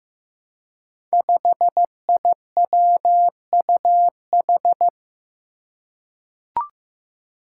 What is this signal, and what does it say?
Morse code, heard as a keyed tone, about 15 wpm, 700 Hz. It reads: 5IWUH
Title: Five Challenging Letter-Number Combinations 15wpm